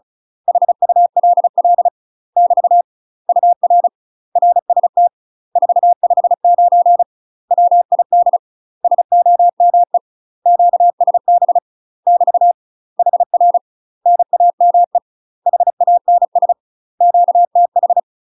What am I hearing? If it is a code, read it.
Morse code, heard as a keyed tone, about 36 wpm, 700 Hz. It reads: HULL <BT> UR RST 459 WID SOME QSB <BT> HR NAME HANS QTH
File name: UK MASTER QSOs TRAINING / 35WPM / UK MASTER QSOs 0038